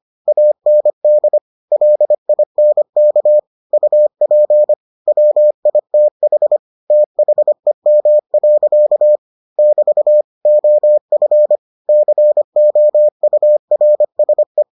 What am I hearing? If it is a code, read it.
Morse code, heard as a keyed tone, about 25 wpm, 600 Hz. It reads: AND LINK UP WITH THEM. <BT> OF COURSE